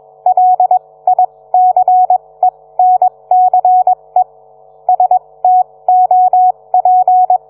Morse code, heard as a keyed tone, about 21 wpm, spaced as for 18 wpm, 740 Hz. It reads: LICENCE STOP